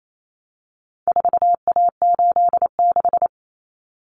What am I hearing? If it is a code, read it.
Morse code, heard as a keyed tone, about 28 wpm, 700 Hz. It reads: 4A86
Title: Common Cut Numbers - 4-Digit 28wpm